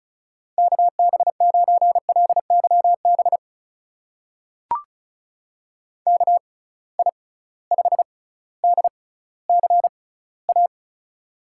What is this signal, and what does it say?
Morse code, heard as a keyed tone, about 36 wpm, spaced as for 18 wpm, 700 Hz. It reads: KB9LYB KI5DCA
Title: USA Call Signs - 6x Character Spacing - 35wpm